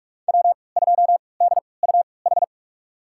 Morse code, heard as a keyed tone, about 43 wpm, spaced as for 25 wpm, 700 Hz. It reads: W2DUH
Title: USA Call Signs - 5 Characters - 3x Character Spacing - 45wpm